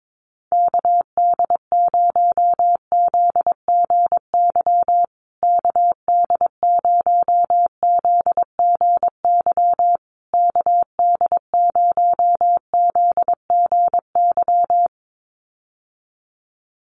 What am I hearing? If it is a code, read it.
Morse code, heard as a keyed tone, about 22 wpm, 700 Hz. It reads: KD0ZGY KD0ZGY KD0ZGY
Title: USA Call Signs Repeated 3x 22wpm